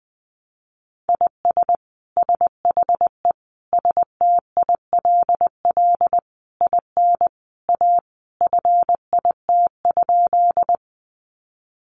Morse code, heard as a keyed tone, about 20 wpm, 700 Hz. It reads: IS SHE STILL IN A FIT?
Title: Sentences from Top 800 Words 20wpm